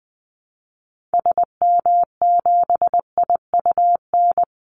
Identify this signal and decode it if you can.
Morse code, heard as a keyed tone, about 20 wpm, 700 Hz. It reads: SM7IUN
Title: Call Signs Repeated 3x - 20wpm